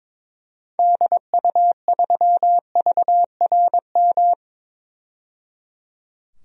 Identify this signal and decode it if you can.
Morse code, heard as a keyed tone, about 21 wpm, 700 Hz. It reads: DU3VRM
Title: DXCC-100 Call Signs - 6 Characters - 22wpm